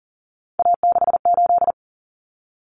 Morse code, heard as a keyed tone, about 40 wpm, 700 Hz. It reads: A68
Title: Common Cut Numbers - 3-Digit 40wpm